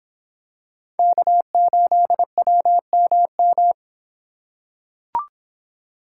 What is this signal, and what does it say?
Morse code, heard as a keyed tone, about 26 wpm, 700 Hz. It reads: K8WMM